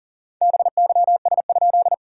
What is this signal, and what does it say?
Morse code, heard as a keyed tone, about 40 wpm, 690 Hz. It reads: BYS?
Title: SOTA QSO Practice 40wpm